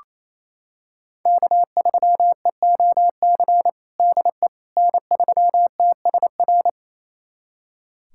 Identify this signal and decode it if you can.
Morse code, heard as a keyed tone, about 28 wpm, 700 Hz. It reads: K3EOC DE N3TSR
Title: USA Callsign DE Callsign - 5 Characters - 28wpm